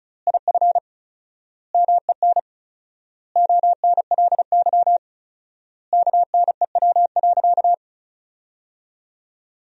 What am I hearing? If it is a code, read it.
Morse code, heard as a keyed tone, about 34 wpm, 700 Hz. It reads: IF MEN ONLY KNEW.
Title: 4x Word-Spacing - Sentences from Top 500 Words - 35wpm